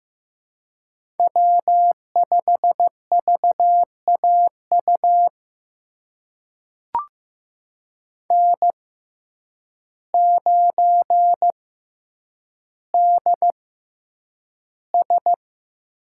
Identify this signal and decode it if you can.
Morse code, heard as a keyed tone, about 15 wpm, spaced as for 8 wpm, 700 Hz. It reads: W5VAU N9DS
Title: USA Call Signs - 5 Characters - 6x Character Spacing - 15wpm